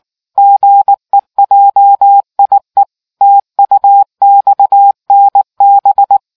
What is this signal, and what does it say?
Morse code, heard as a keyed tone, about 19 wpm, 800 Hz. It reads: GEJIE TUXNB